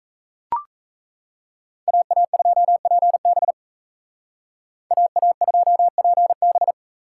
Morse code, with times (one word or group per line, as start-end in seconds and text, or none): 1.9-3.5 AA2PB
4.9-6.7 AA2PB